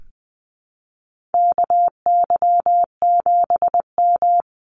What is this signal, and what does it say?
Morse code, heard as a keyed tone, about 20 wpm, 700 Hz. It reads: KY7M
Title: Call Signs Repeated 3x - 20wpm